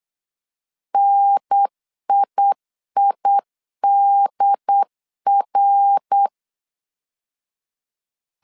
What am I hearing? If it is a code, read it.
Morse code, heard as a keyed tone, about 8 wpm, 800 Hz. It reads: NIIDR